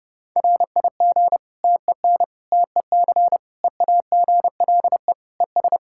Morse code, heard as a keyed tone, about 30 wpm, 700 Hz. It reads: RIG TEN TEC EAGLE ES
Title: Practice QSO - Exchange 2 - Info - 30wpm